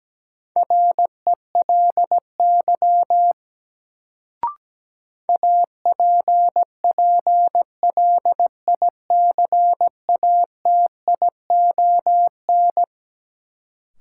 Morse code, heard as a keyed tone, about 17 wpm, 700 Hz. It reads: RELY APPLICATION
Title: Sentences from Top 1200 Words - New Word Drill Alt - 17wpm